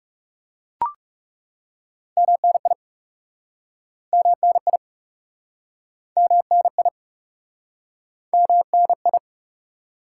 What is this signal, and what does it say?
Morse code, heard as a keyed tone, about 40 wpm, 700 Hz. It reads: MNI MNI MNI MNI